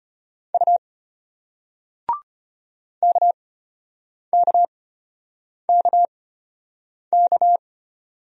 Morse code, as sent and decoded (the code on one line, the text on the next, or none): ..- / -.- -.- -.- -.-
U KKKK